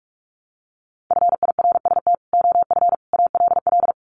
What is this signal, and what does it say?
Morse code, heard as a keyed tone, about 44 wpm, 700 Hz. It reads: FIRST OF ALL
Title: Sets of 3 Words 45wpm